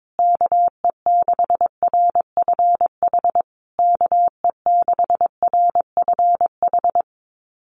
Morse code, heard as a keyed tone, about 22 wpm, 700 Hz. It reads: KE6RFH KE6RFH
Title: USA Call Signs Repeated 3x 22wpm